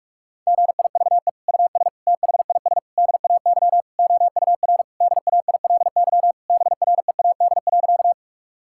Morse code, heard as a keyed tone, about 46 wpm, 700 Hz. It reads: GIVE US THIS DAY OUR DAILY BREAD.
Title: Sentences from Top 1600 Words 45wpm